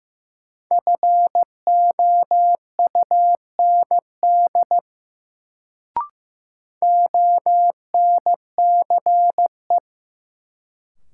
Morse code, heard as a keyed tone, about 15 wpm, 700 Hz. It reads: FOUND ONCE
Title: Sentences from Top 300 Words - New Word Drill - 15wpm